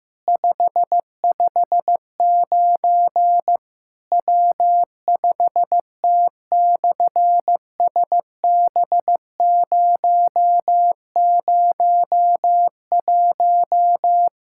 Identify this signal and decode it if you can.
Morse code, heard as a keyed tone, about 15 wpm, 700 Hz. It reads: 559 W5T/SB001
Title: SOTA QSO Practice 15wpm